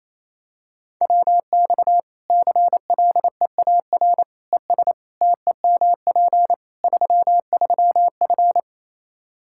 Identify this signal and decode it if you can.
Morse code, heard as a keyed tone, about 27 wpm, 700 Hz. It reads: WX CLEAR ES TEMP 33F
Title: Practice QSO - Exchange 2 - Info - 28wpm